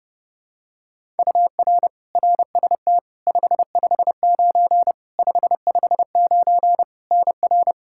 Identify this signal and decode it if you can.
Morse code, heard as a keyed tone, about 30 wpm, 700 Hz. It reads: UR RST 559 559 NR
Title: Practice QSO - Exchange 1 - Beginning - 30wpm